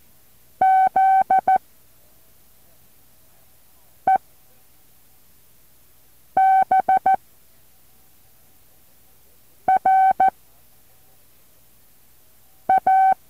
Morse code, heard as a keyed tone, about 14 wpm, 760 Hz. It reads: Z E B R A
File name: intl morse code2